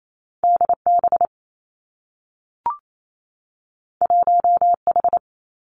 Morse code, heard as a keyed tone, about 28 wpm, 700 Hz. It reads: D B 1 H
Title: Two Challenging Letter-Number Combinations 28wpm